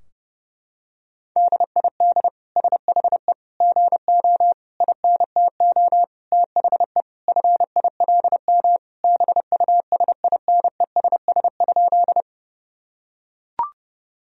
Morse code, heard as a keyed tone, about 30 wpm, 700 Hz. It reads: DID SHE GO INTO THE FILM BUSINESS?
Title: Sentences from Top 1000 Words - Review Part 3 - 30wpm